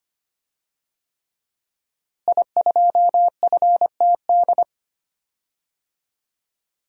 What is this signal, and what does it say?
Morse code, heard as a keyed tone, about 25 wpm, 700 Hz. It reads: I2FTD
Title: DXCC-100 Call Signs - 5 Characters - Repeated 3x - 25wpm